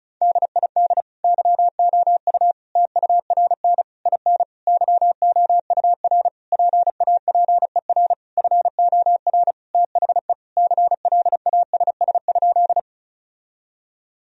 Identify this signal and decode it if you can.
Morse code, heard as a keyed tone, about 36 wpm, 700 Hz. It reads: DID YOU TURN IN YOUR PAPER FOR THE CLASS?
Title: Sentences from Top 1000 Words - Review Part 2 - 35wpm